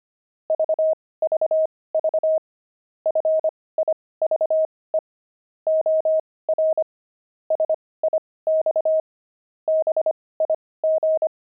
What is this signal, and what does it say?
Morse code, heard as a keyed tone, about 25 wpm, spaced as for 18 wpm, 630 Hz. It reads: VVV FIVE OR SIX BIG
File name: pangram-fs-18-cs-25